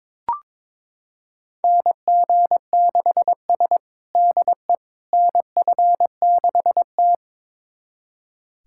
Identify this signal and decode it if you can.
Morse code, heard as a keyed tone, about 22 wpm, 700 Hz. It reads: NG6S DE NF6T
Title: USA Callsign DE Callsign - 4 Characters - 22wpm